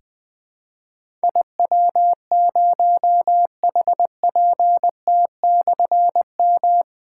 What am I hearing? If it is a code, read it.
Morse code, heard as a keyed tone, about 20 wpm, 700 Hz. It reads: IW0HPT/M
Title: Difficult Call Signs 20wpm